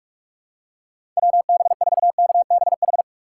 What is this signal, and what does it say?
Morse code, heard as a keyed tone, about 43 wpm, 700 Hz. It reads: WB4KBH
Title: USA Call Signs - 6 Characters - 6x Character Spacing - 45wpm